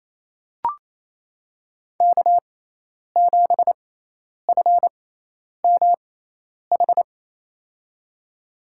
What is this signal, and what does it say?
Morse code, heard as a keyed tone, about 28 wpm, spaced as for 10 wpm, 700 Hz. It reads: K7FMH
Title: USA Call Signs - 5 Characters - 6x Character Spacing - 28wpm